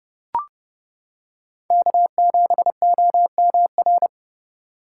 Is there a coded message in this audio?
Yes: Morse code from 1.7 to 4.1 s, heard as a keyed tone, about 29 wpm, 700 Hz. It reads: K7OMR